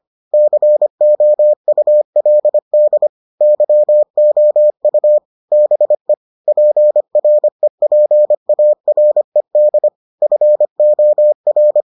Morse code, heard as a keyed tone, about 25 wpm, 600 Hz. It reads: COULD YOU BE PREPARED FOR